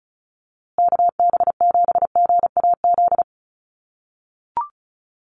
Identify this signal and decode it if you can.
Morse code, heard as a keyed tone, about 34 wpm, 700 Hz. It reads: KB7GAZ
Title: USA Call Signs Repeated 2x 35wpm